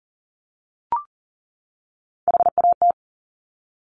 Morse code, heard as a keyed tone, about 38 wpm, 700 Hz. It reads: HAT
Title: Top Three Letter Combinations in Words 40wpm